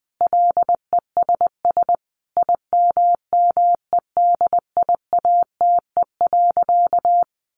Morse code, heard as a keyed tone, about 20 wpm, 700 Hz. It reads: LESS IMMEDIATE.
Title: Sentences from Top 1300 Words 20wpm